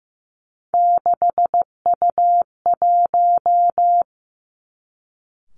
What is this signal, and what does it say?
Morse code, heard as a keyed tone, about 15 wpm, 700 Hz. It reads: 6U1
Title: Three Challenging Letter-Number Combinations 15wpm